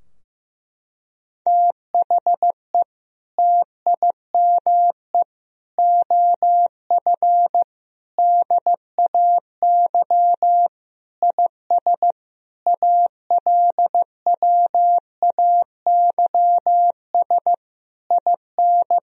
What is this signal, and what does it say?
Morse code, heard as a keyed tone, about 16 wpm, 700 Hz. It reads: THE TIME OF DAY IS ALWAYS IN
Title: Sentences from Top 1000 Words - Review Part 3 - 15wpm